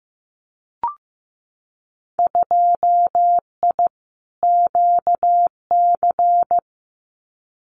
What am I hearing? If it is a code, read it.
Morse code, heard as a keyed tone, about 15 wpm, 700 Hz. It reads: 2I QC
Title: Winter Field Day Exchanges 15wpm